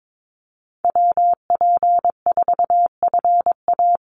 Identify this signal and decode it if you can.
Morse code, heard as a keyed tone, about 22 wpm, 700 Hz. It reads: WP4FA
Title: USA Call Signs 22wpm